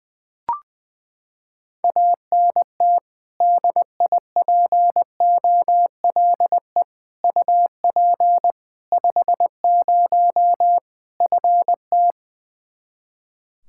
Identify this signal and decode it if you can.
Morse code, heard as a keyed tone, about 20 wpm, 700 Hz. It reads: ANT DIPOLE UP 50 FT
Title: K1USN SST QSO Practice 20wpm